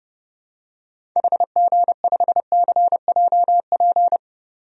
Morse code, heard as a keyed tone, about 32 wpm, 700 Hz. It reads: HG5CJP
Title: DXCC-100 Call Signs - 6 Characters - 30wpm